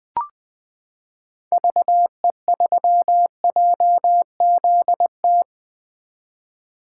Morse code, heard as a keyed tone, about 20 wpm, 700 Hz. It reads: VE3JZT
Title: 2023 POTA Activator Call Signs - 20wpm